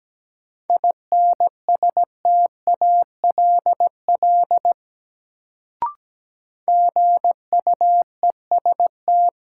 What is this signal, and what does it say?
Morse code, heard as a keyed tone, about 17 wpm, 700 Hz. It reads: INSTALL GUEST
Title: Sentences from Top 1600 Words - New Word Drill - 17wpm